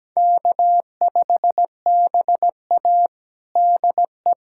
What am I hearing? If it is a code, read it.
Morse code, heard as a keyed tone, about 17 wpm, 700 Hz. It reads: K5BA DE
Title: USA Callsign DE Callsign - 4 Characters - 17wpm